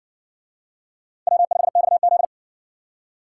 Morse code, heard as a keyed tone, about 56 wpm, 700 Hz. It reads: W5XZ